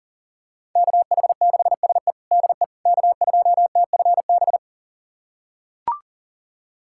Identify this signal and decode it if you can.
Morse code, heard as a keyed tone, about 40 wpm, 700 Hz. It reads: KH6SE DE K2TFB
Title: USA Callsign DE Callsign - 5 Characters - 40wpm